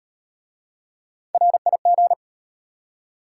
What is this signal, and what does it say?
Morse code, heard as a keyed tone, about 31 wpm, 690 Hz. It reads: RIG
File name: Speed-Racing Top 20 QSO Elements - 25wpm